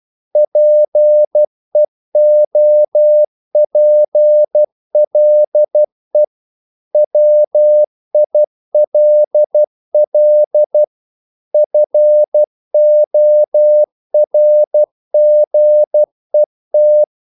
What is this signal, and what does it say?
Morse code, heard as a keyed tone, about 12 wpm, 600 Hz. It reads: PEOPLE WILL FORGET
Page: English quotations in CW (telegraphy - morse)